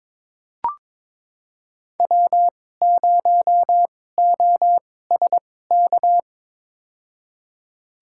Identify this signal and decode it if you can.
Morse code, heard as a keyed tone, about 21 wpm, spaced as for 16 wpm, 700 Hz. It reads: W0OSK
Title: USA Call Signs - 5 Characters - 2x Character Spacing - 22wpm